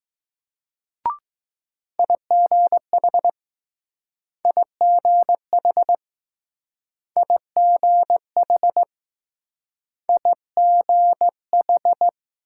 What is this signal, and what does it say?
Morse code, heard as a keyed tone, about 20 wpm, 700 Hz. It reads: IGH IGH IGH IGH